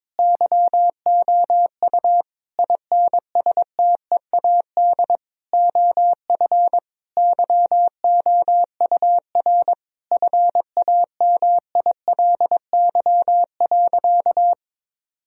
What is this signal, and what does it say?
Morse code, heard as a keyed tone, about 22 wpm, 700 Hz. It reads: YOU INSTEAD OF YOUR FAMILY.
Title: Sentences from Top 1000 Words - Review Part 4 - 22wpm